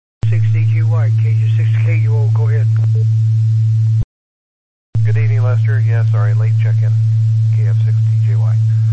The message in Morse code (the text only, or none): none